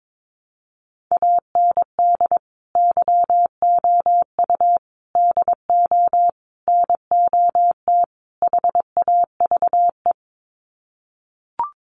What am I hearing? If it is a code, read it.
Morse code, heard as a keyed tone, about 22 wpm, 700 Hz. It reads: AND YOU DO NOT HAVE
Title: Sets of 5 Words Encore 22wpm